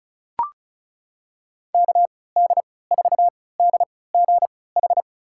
Morse code, heard as a keyed tone, about 34 wpm, spaced as for 20 wpm, 700 Hz. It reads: KD4DGH
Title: USA Call Signs - 3x Character Spacing - 35wpm